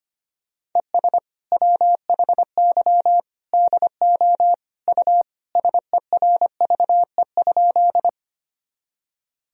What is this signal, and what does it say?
Morse code, heard as a keyed tone, about 25 wpm, 700 Hz. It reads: ES WHY DO U SERVE?
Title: Sentences from Top 400 Words and Abbreviations 25wpm